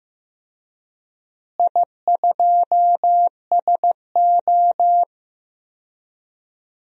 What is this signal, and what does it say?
Morse code, heard as a keyed tone, about 15 wpm, 700 Hz. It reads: I2SO
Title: DXCC-100 Call Signs - 4 Characters - Repeated 3x - 15wpm